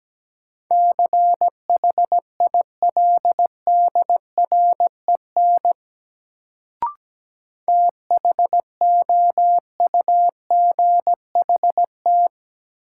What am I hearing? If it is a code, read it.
Morse code, heard as a keyed tone, about 17 wpm, 700 Hz. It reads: CHILDREN THOUGHT
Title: Sentences from Top 300 Words - New Word Drill - 17wpm